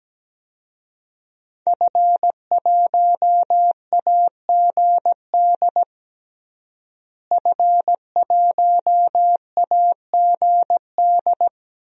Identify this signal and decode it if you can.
Morse code, heard as a keyed tone, about 17 wpm, 700 Hz. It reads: F1AGD F1AGD
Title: DXCC-100 Call Signs - 5 Characters - Repeated 3x - 17wpm